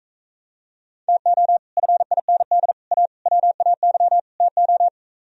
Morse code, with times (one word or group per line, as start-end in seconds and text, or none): 1.1-1.6 TO
1.8-2.7 FIND
2.9-3.1 A
3.3-4.2 WAY
4.4-4.9 TO